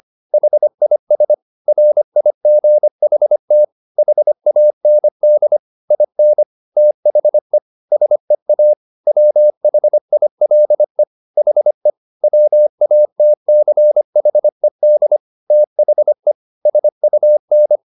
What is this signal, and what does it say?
Morse code, heard as a keyed tone, about 25 wpm, 600 Hz. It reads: HIS RIGHT HAND IN THE SEA WHILE HE WATCHED THE SUN